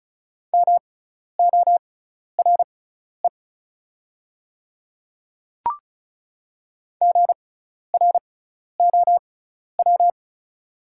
Morse code, heard as a keyed tone, about 34 wpm, spaced as for 12 wpm, 690 Hz. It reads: MORE GROW
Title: 6x Character Spacing - Top 300 Words - 35wpm